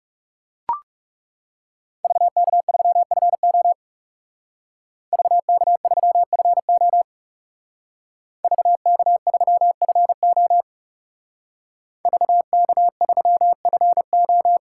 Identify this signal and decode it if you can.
Morse code, heard as a keyed tone, about 40 wpm, 700 Hz. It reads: VK3FO VK3FO VK3FO VK3FO